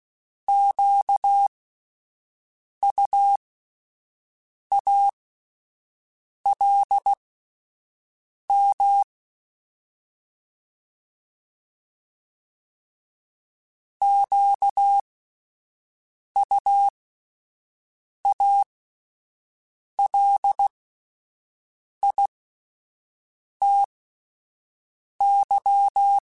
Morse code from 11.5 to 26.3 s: QUALITY

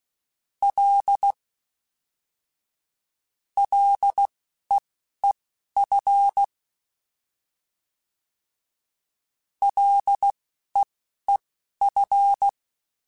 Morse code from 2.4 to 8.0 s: LEEF

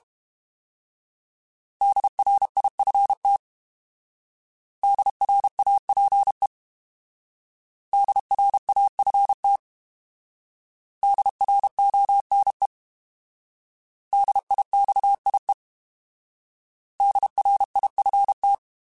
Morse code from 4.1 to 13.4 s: DRAPE DRAFT DRONE

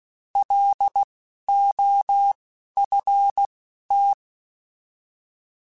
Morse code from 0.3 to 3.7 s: L O F

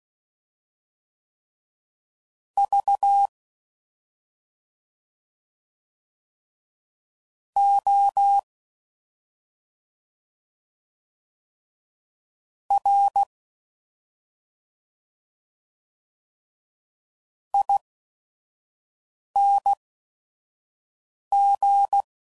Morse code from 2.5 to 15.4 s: V O R

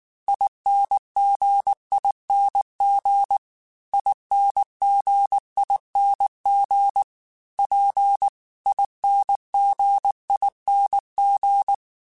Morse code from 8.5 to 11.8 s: INGING

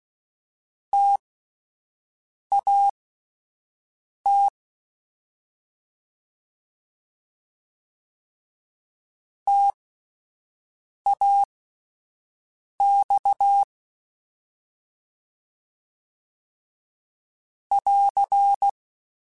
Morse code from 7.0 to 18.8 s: TAX <AR>

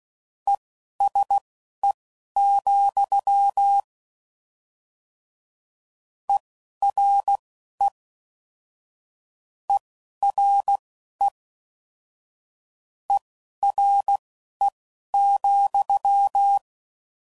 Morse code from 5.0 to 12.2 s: ERE ERE

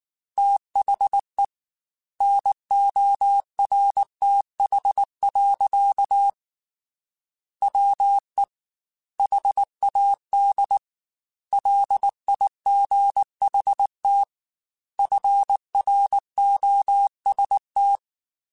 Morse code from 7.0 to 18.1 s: WE HAD LIGHT FROST

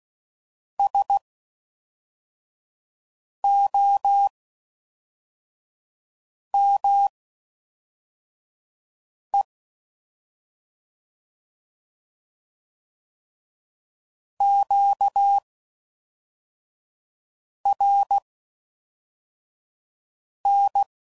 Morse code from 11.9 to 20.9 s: QRN

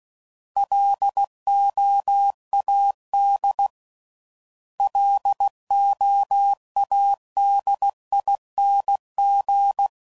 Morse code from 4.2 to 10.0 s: LOADING

